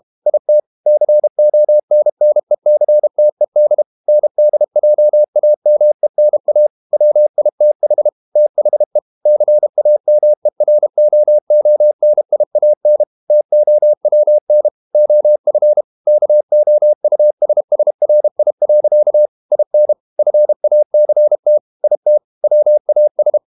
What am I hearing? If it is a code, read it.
Morse code, heard as a keyed tone, about 32 wpm, 600 Hz. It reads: IT CONNECTED NDJAMENA WITH THE CAMEROONIAN TOWN OF KOUSSRI. IN FACT IT WAS
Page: Never 31wpm | CW med Gnister